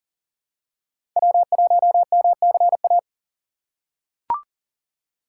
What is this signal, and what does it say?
Morse code, heard as a keyed tone, about 40 wpm, 700 Hz. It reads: W1MCA
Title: USA Call Signs - 5 Characters - 3x Character Spacing - 40wpm